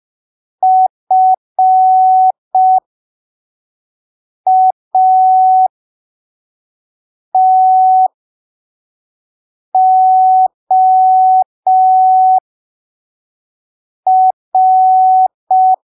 Morse code, heard as a keyed tone, about 5 wpm, spaced as for 3 wpm, 750 Hz. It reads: FATOR
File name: arquivos / texto / texto 09 05ppm